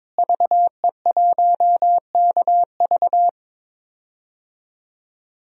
Morse code, heard as a keyed tone, about 22 wpm, 700 Hz. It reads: VE1KV
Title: DXCC-100 Call Signs - 5 Characters - Repeated 2x - 22wpm